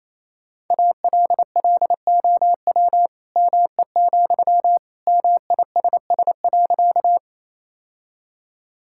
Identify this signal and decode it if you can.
Morse code, heard as a keyed tone, about 27 wpm, 700 Hz. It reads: ALLOW ME, MISS.
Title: Sentences from Top 1000 Words - Review Part 5 - 28wpm